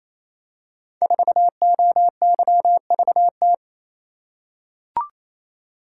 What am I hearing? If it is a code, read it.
Morse code, heard as a keyed tone, about 27 wpm, 700 Hz. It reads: 4OYVT
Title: Code Groups - 5 Letter:Numbers - 28wpm